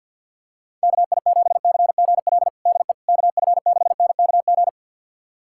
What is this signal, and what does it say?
Morse code, heard as a keyed tone, about 52 wpm, 700 Hz. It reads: KI7CGL DE KF6NKG
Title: USA Callsign DE Callsign - 6 Characters - 50wpm